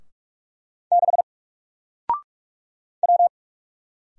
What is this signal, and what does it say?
Morse code, heard as a keyed tone, about 44 wpm, 700 Hz. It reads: 6 W
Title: Single Letter-Number 45wpm